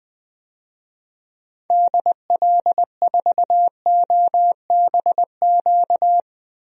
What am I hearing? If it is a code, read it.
Morse code, heard as a keyed tone, about 20 wpm, 700 Hz. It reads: DL4OBQ
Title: USA Call Signs - 6 Characters - Repeated 3x - 20wpm